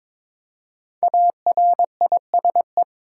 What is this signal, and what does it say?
Morse code, heard as a keyed tone, about 22 wpm, 700 Hz. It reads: ARISE
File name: 5 Letter Words - First 3 Letters Given - 22wpm